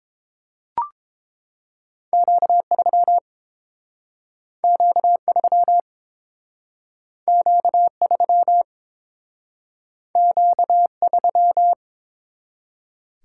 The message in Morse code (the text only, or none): Q3 Q3 Q3 Q3